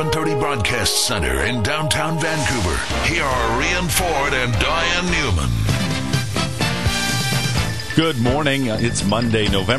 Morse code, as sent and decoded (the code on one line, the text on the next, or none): none